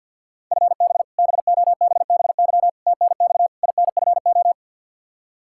Morse code, heard as a keyed tone, about 51 wpm, 700 Hz. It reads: FB BOBBY TNX INFO
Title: Practice QSO - Exchange 3 - More Info - 50wpm